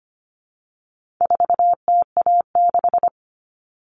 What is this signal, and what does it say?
Morse code, heard as a keyed tone, about 25 wpm, 700 Hz. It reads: 4TA6
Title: All Cut Numbers - 4-Digit 25wpm